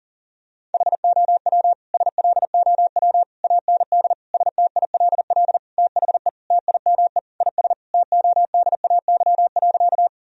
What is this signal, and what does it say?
Morse code, heard as a keyed tone, about 40 wpm, 700 Hz. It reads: HOW SLOW AND STILL THE TIME IS TODAY.
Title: Sentences from Top 1000 Words - Review Part 2 - 40wpm